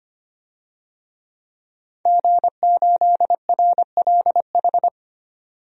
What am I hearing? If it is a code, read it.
Morse code, heard as a keyed tone, about 25 wpm, 700 Hz. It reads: G8RLH
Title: DXCC-100 Call Signs - 5 Characters - 25wpm